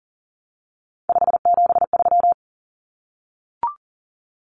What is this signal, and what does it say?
Morse code, heard as a keyed tone, about 40 wpm, 700 Hz. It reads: 573
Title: Numbers - 3-Digit 40wpm